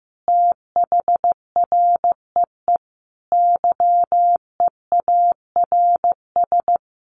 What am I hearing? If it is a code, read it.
Morse code, heard as a keyed tone, about 15 wpm, 700 Hz. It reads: THREE YEARS
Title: Sets of 5 Words Encore2 15wpm